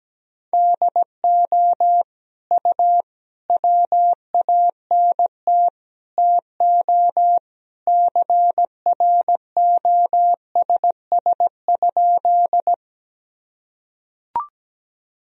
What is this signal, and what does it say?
Morse code, heard as a keyed tone, about 17 wpm, 700 Hz. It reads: DO U WANT TO CROSS?
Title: Sentences from Top 300 Words and Abbreviations 17wpm